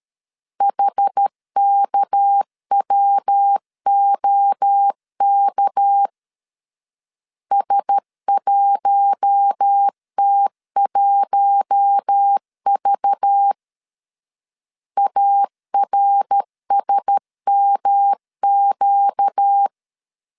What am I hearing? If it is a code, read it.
Morse code, heard as a keyed tone, about 13 wpm, 800 Hz. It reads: HKWOK S1T1V ARSMQ
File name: LEZIONI / 6 PKX0 / 6 PK 12wpm 2006